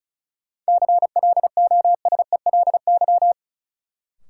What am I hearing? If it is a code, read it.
Morse code, heard as a keyed tone, about 36 wpm, 700 Hz. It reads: CLOSELY
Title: Sentences from Top 1400 Words - New Word Drill Alt - 35wpm